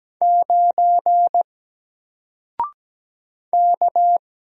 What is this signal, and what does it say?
Morse code, heard as a keyed tone, about 17 wpm, 700 Hz. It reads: 9 K
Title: Single Character (Letter:Number:Punctuation:Prosign) Alt 17wpm